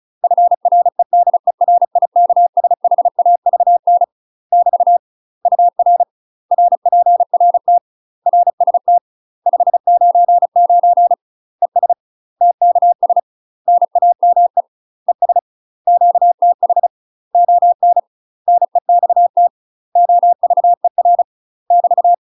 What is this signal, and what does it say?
Morse code, heard as a keyed tone, about 36 wpm, 700 Hz. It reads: FREDERIKSHAVN <BT> UR RPRT RST 599 ES TKS NAME ES QTH ON NEXT OVER <BT>
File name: UK MASTER QSOs TRAINING / 35WPM / UK MASTER QSOs 0016